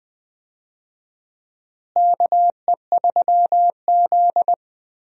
Morse code, heard as a keyed tone, about 20 wpm, 700 Hz. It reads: KE3Z
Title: USA Call Signs - 4 Characters - Repeated 3x - 20wpm